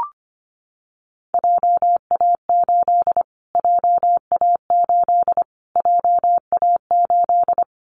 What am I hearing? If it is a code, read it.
Morse code, heard as a keyed tone, about 25 wpm, 700 Hz. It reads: JA8 JA8 JA8
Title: Callsign Prefixes for Top 25 DXCC - Repeated 3x - 25wpm